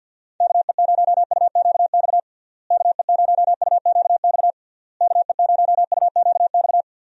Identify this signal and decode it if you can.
Morse code, heard as a keyed tone, about 51 wpm, 700 Hz. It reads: KE0UQX KE0UQX KE0UQX
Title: USA Call Signs - 6 Characters - Repeated 3x - 50wpm